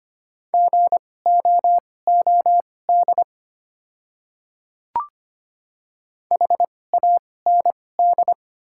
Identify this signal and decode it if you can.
Morse code, heard as a keyed tone, about 25 wpm, spaced as for 18 wpm, 700 Hz. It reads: GOOD HAND
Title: 2x Character Spacing - Top 200 Words - 25wpm